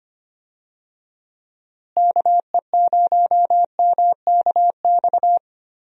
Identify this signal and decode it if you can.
Morse code, heard as a keyed tone, about 25 wpm, 700 Hz. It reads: KE0MKX
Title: USA Call Signs - 6 Characters - Repeated 3x - 25wpm